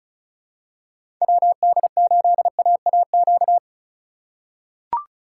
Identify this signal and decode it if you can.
Morse code, heard as a keyed tone, about 36 wpm, 700 Hz. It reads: WD8AAQ
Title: USA Call Signs - 6 Characters - 2x Character Spacing - 35wpm